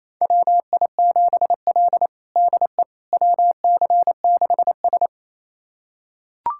Morse code, heard as a keyed tone, about 27 wpm, 700 Hz. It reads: WI7L DE WC6S
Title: USA Callsign DE Callsign - 4 Characters - 28wpm